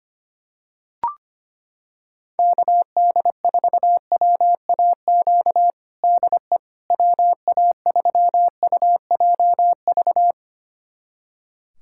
Morse code, heard as a keyed tone, about 25 wpm, 700 Hz. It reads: KD4WAQ DE WA3UJV
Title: USA Callsign DE Callsign 25wpm